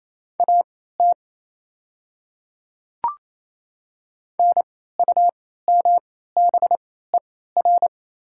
Morse code, heard as a keyed tone, about 28 wpm, spaced as for 16 wpm, 700 Hz. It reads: AT NUMBER
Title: 3x Character Spacing - Top 100 Words - 28wpm